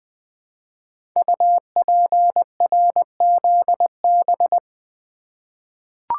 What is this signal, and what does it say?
Morse code, heard as a keyed tone, about 20 wpm, 700 Hz. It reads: UPRZB
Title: Code Groups - 5 Letters - 20wpm